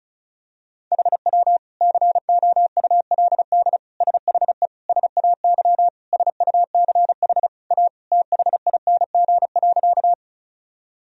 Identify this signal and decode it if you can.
Morse code, heard as a keyed tone, about 36 wpm, 700 Hz. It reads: HW COULD SHE SAY SUCH A THING.
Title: Sentences from Top 200 Words and Abbreviations 35wpm